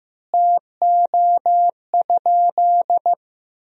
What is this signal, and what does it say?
Morse code, heard as a keyed tone, about 16 wpm, 700 Hz. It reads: TO?
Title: 2x Word-Spacing - Sentences from Top 200 Words - 15wpm